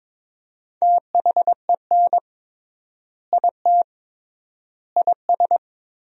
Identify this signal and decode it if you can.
Morse code, heard as a keyed tone, about 22 wpm, 700 Hz. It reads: THEN IT IS
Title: 3x Word-Spacing - Sentences from Top 500 Words - 22wpm